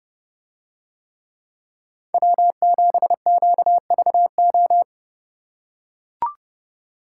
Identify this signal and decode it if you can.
Morse code, heard as a keyed tone, about 29 wpm, 700 Hz. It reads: W7QVO